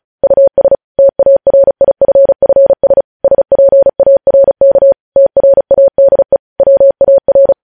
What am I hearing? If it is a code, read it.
Morse code, heard as a keyed tone, about 36 wpm, 550 Hz. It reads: US TARIFFS SPARK TRADE WAR
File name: podcast / News headlines-1x-35x35